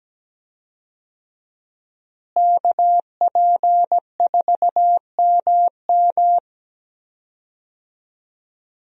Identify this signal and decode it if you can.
Morse code, heard as a keyed tone, about 17 wpm, 700 Hz. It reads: KP4MM
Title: USA Call Signs - 5 Characters - Repeated 2x - 17wpm